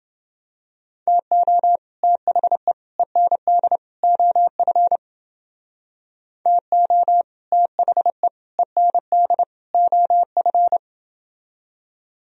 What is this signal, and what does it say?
Morse code, heard as a keyed tone, about 29 wpm, 700 Hz. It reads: TO THE END OF TO THE END OF